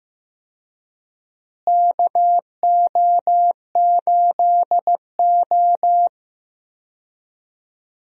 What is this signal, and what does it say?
Morse code, heard as a keyed tone, about 15 wpm, 700 Hz. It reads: KO8O
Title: USA Call Signs - 4 Characters - Repeated 2x - 15wpm